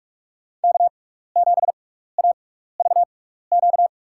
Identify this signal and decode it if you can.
Morse code, heard as a keyed tone, about 41 wpm, spaced as for 15 wpm, 700 Hz. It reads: K7AVQ
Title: USA Call Signs - 5 Characters - 6x Character Spacing - 45wpm